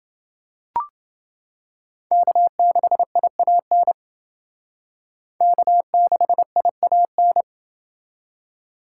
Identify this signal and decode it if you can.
Morse code, heard as a keyed tone, about 28 wpm, 700 Hz. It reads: K6IAN K6IAN